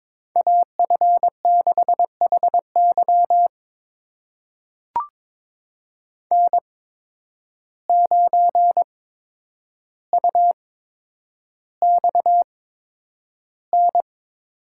Morse code, heard as a keyed tone, about 22 wpm, spaced as for 9 wpm, 700 Hz. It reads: AF6HY N9UXN
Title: USA Call Signs - 5 Characters - 8x Character Spacing - 22wpm